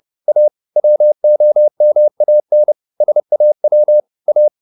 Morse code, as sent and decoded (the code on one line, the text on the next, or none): .- / .-- --- -- .- -. / ... .- .-- / .-
A WOMAN SAW A